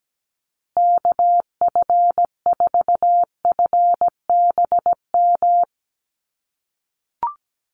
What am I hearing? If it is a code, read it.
Morse code, heard as a keyed tone, about 17 wpm, 700 Hz. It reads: KF4FBM
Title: USA Call Signs Repeated 2x 17wpm